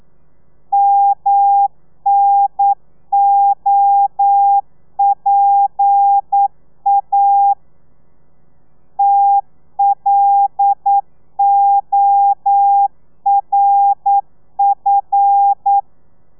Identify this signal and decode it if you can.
Morse code, heard as a keyed tone, about 9 wpm, 800 Hz. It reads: MNOPA TLORF
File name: telegrafia / i2tfj / Lez11 40